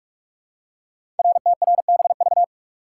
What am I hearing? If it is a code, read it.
Morse code, heard as a keyed tone, about 44 wpm, 700 Hz. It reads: RTRBV
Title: Code Groups - 5 Letters - 45wpm